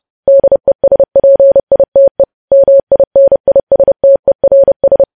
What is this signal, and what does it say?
Morse code, heard as a keyed tone, about 30 wpm, 550 Hz. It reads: DESPITE MINISTERS
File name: podcast / News headlines-1x-30x30